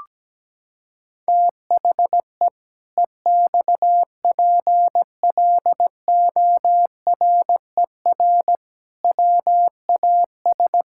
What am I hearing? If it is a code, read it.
Morse code, heard as a keyed tone, about 17 wpm, 700 Hz. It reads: THE EXPLORER WAS
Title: Pangram Challenge v2 - 17wpm